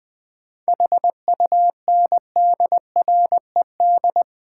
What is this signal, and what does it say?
Morse code, heard as a keyed tone, about 20 wpm, 700 Hz. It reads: HUNDRED
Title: Sentences from Top 400 Words - New Word Drill - 20wpm